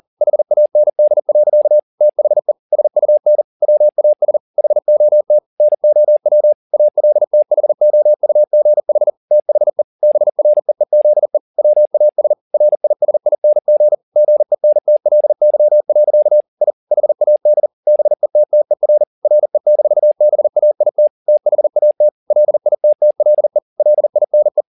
Morse code, heard as a keyed tone, about 40 wpm, 600 Hz. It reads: HAND. THE SUN WAS HOT NOW ALTHOUGH THE BREEZE WAS RISING GENTLY. I HAD BETTER RE-BAIT THAT LITTLE LINE